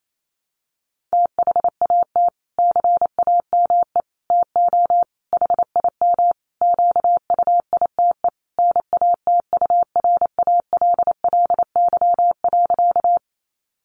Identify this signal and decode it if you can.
Morse code, heard as a keyed tone, about 28 wpm, 700 Hz. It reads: THAT CAME TO HIM QUITE NATURALLY.
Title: Sentences from Top 1300 Words 28wpm